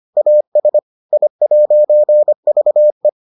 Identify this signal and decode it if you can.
Morse code, heard as a keyed tone, about 25 wpm, 600 Hz. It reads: AS I'VE